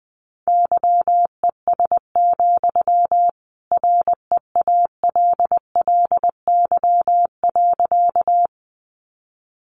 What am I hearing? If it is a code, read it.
Morse code, heard as a keyed tone, about 20 wpm, 700 Hz. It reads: YES, REALLY.
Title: Sentences from Top 600 Words 20wpm